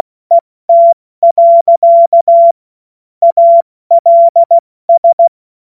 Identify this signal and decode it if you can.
Morse code, heard as a keyed tone, about 16 wpm, 680 Hz. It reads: ET. ALS
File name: Morsekurs / Witze / Witz 09. August 2022 70 F80